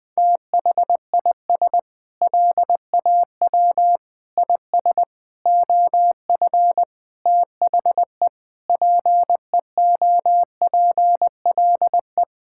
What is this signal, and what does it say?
Morse code, heard as a keyed tone, about 21 wpm, 700 Hz. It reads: THIS LAW IS OF THE PEOPLE
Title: Sentences with 9-Word Phrases - 20wpm